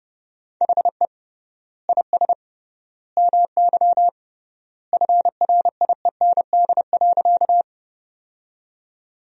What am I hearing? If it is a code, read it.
Morse code, heard as a keyed tone, about 32 wpm, 700 Hz. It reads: HE IS MY FRIEND.
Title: 3x Word-Spacing - Sentences from Top 300 Words - 30wpm